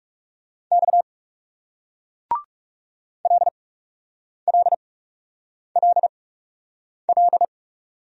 Morse code, heard as a keyed tone, about 40 wpm, 700 Hz. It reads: X L L L L